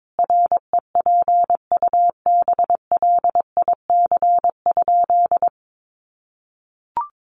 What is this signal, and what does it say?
Morse code, heard as a keyed tone, about 22 wpm, 700 Hz. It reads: REPUBLIC?
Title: Sentences from Top 1500 Words 22wpm